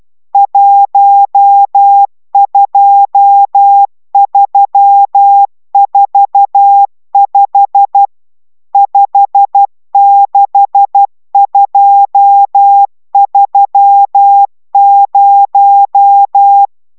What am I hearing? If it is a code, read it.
Morse code, heard as a keyed tone, about 12 wpm, 800 Hz. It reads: 12345 56230